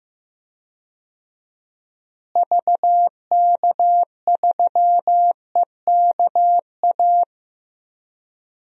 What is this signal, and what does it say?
Morse code, heard as a keyed tone, about 16 wpm, 700 Hz. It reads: VK3EKA
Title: DXCC-100 Call Signs - 6 Characters - Repeated 2x - 15wpm